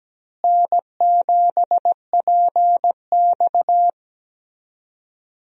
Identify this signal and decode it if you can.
Morse code, heard as a keyed tone, about 17 wpm, 700 Hz. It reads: N7PX
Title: USA Call Signs - 4 Characters - Repeated 2x - 17wpm